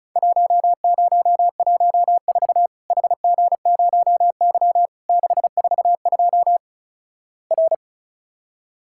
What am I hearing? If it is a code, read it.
Morse code, heard as a keyed tone, about 36 wpm, 700 Hz. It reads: 1014 HG0Y 642 R
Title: Worked All Europe DX Contest - QTC Exchange - 35wpm